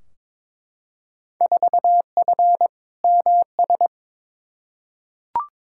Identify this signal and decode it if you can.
Morse code, heard as a keyed tone, about 22 wpm, 700 Hz. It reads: 4F MS
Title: Field Day Exchanges Encore - 22wpm